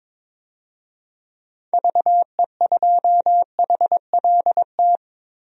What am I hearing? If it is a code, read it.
Morse code, heard as a keyed tone, about 22 wpm, 700 Hz. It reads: VE2HLT
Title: DXCC-100 Call Signs - 6 Characters - 22wpm